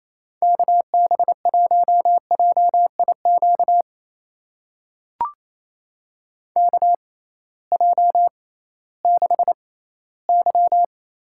Morse code, heard as a keyed tone, about 28 wpm, spaced as for 16 wpm, 700 Hz. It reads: KB1JIQ KJ6Y
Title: USA Call Signs - 6 Characters - 6x Character Spacing - 28wpm